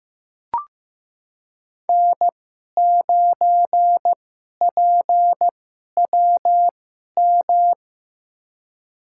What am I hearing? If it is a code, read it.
Morse code, heard as a keyed tone, about 15 wpm, spaced as for 11 wpm, 700 Hz. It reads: N9PWM
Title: USA Call Signs - 2x Character Spacing - 15wpm